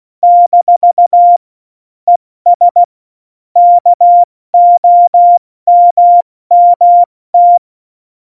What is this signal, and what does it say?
Morse code, heard as a keyed tone, about 16 wpm, 700 Hz. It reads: - ES KOMMT
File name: Morsekurs / Witze / Witz 03. November 2020, 70 F80